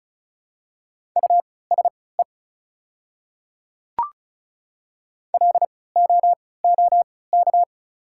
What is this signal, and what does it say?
Morse code, heard as a keyed tone, about 36 wpm, spaced as for 20 wpm, 690 Hz. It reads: USE LOOK
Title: 3x Character Spacing - Top 200 Words - 35wpm